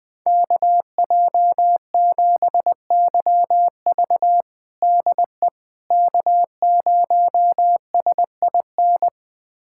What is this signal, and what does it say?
Morse code, heard as a keyed tone, about 21 wpm, 700 Hz. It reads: KJ7YV DE K0SIN
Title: USA Callsign DE Callsign - 5 Characters - 20wpm